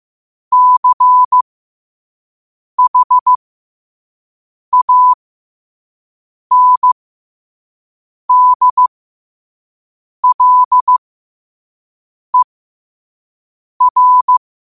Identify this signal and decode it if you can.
Morse code, heard as a keyed tone, about 15 wpm, spaced as for 5 wpm, 1000 Hz. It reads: CHANDLER
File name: qotdfiles / 5wpm